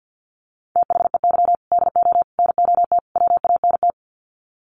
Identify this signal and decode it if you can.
Morse code, heard as a keyed tone, about 50 wpm, 700 Hz. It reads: THEY DO NOT WANT
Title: Sets of 4 Words 50wpm